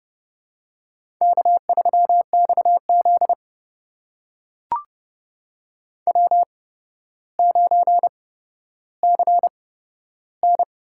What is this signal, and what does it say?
Morse code, heard as a keyed tone, about 30 wpm, spaced as for 13 wpm, 700 Hz. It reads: K3XZ W9CN
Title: USA Call Signs - 4 Characters - 8x Character Spacing - 30wpm